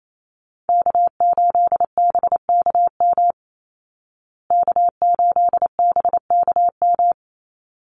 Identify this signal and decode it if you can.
Morse code, heard as a keyed tone, about 28 wpm, 700 Hz. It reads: K8BKM K8BKM
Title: Call Signs Repeated 3x - 28wpm